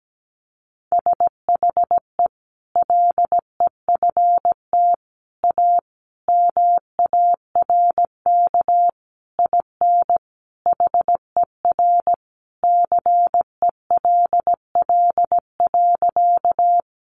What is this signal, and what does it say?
Morse code, heard as a keyed tone, about 17 wpm, 700 Hz. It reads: SHE LEFT A MARK IN HER CELL.
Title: Sentences from Top 1100 Words 17wpm